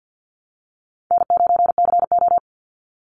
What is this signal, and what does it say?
Morse code, heard as a keyed tone, about 50 wpm, 700 Hz. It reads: N9CO
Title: Easy Call Signs - 50wpm